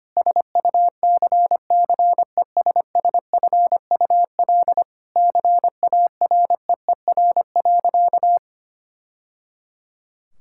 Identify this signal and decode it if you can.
Morse code, heard as a keyed tone, about 25 wpm, 700 Hz. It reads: SUCCESSFUL CAREER.
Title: Sentences from Top 1000 Words - Review Part 6 - 25wpm